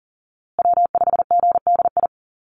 Morse code, heard as a keyed tone, about 40 wpm, 700 Hz. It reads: W5GDI
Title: USA Call Signs Repeated 2x 40wpm